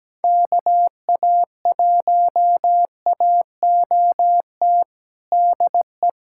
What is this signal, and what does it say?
Morse code, heard as a keyed tone, about 17 wpm, 700 Hz. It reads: KA1AOT DE
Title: USA Callsign DE Callsign 17wpm